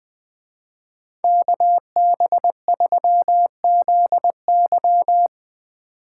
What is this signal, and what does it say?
Morse code, heard as a keyed tone, about 19 wpm, 700 Hz. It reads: KB3ZY
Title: Speed Racing - USA 2x2 Call Signs 17wpm